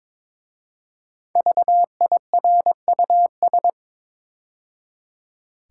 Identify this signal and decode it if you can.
Morse code, heard as a keyed tone, about 22 wpm, 690 Hz. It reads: VIRUS
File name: 5 Letter Words - First 3 Letters Given - 22wpm